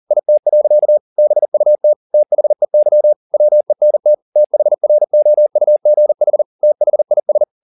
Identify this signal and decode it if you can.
Morse code, heard as a keyed tone, about 40 wpm, 600 Hz. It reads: IT. BUT THEY WENT THROUGH THIS